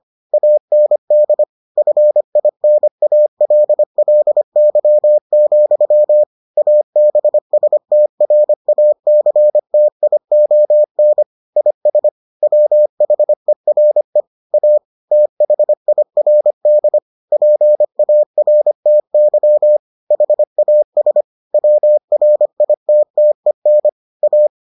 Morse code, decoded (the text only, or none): AND FINALLY, ABSTRACTION IS WHERE A THIRD PARTY HAS WRITTEN A